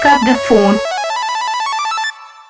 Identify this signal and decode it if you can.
Speech. There is no code.